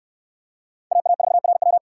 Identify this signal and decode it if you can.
Morse code, heard as a keyed tone, about 59 wpm, 700 Hz. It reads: AA4RW